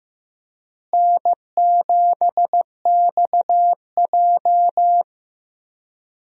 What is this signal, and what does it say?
Morse code, heard as a keyed tone, about 16 wpm, 700 Hz. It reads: N7XJ
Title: USA Call Signs - 4 Characters - Repeated 3x - 15wpm